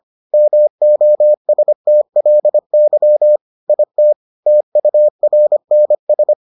Morse code, heard as a keyed tone, about 25 wpm, 600 Hz. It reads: MOSTLY IT TURNS